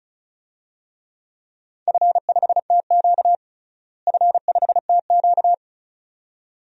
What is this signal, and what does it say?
Morse code, heard as a keyed tone, about 34 wpm, 700 Hz. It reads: F5TQ F5TQ
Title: DXCC-100 Call Signs - 4 Characters - Repeated 2x - 35wpm